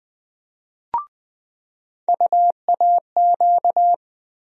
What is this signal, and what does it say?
Morse code, heard as a keyed tone, about 20 wpm, 700 Hz. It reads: UAQ
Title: USA Call Sign Postfixes - 3 Letters 20wpm